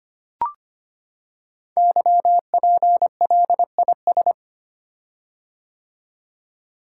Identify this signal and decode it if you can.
Morse code, heard as a keyed tone, about 25 wpm, 700 Hz. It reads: YPLIS
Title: Code Groups - 5 Letters - 25wpm